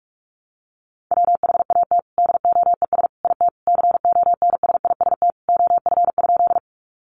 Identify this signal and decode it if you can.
Morse code, heard as a keyed tone, about 44 wpm, 700 Hz. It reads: WHAT DOES IT CONSIST OF?
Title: Sentences from Top 1400 Words 45wpm